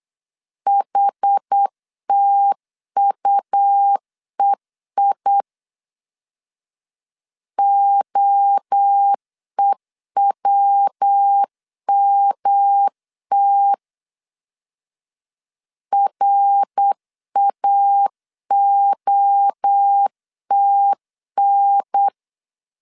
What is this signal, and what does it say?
Morse code, heard as a keyed tone, about 8 wpm, 800 Hz. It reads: HTUEI OEWMT RAOTN
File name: CW / DET / 1 3 1500C S B 8wpm 4141